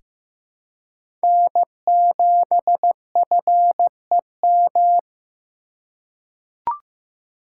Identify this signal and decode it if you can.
Morse code, heard as a keyed tone, about 15 wpm, 700 Hz. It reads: N7FEM
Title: USA Call Signs - 5 Characters - 2x Character Spacing - 15wpm